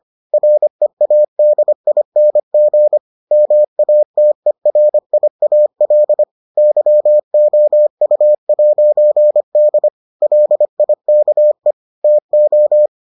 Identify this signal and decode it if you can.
Morse code, heard as a keyed tone, about 25 wpm, 600 Hz. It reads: READING MATERIAL YOU'D LIKE TO